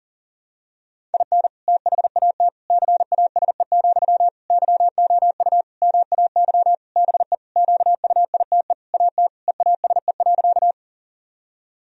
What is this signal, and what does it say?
Morse code, heard as a keyed tone, about 37 wpm, 700 Hz. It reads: IN THAT CASE, YOU MAY BE QUITE AT EASE.
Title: Sentences from Top 1000 Words - Review Part 1 - 40wpm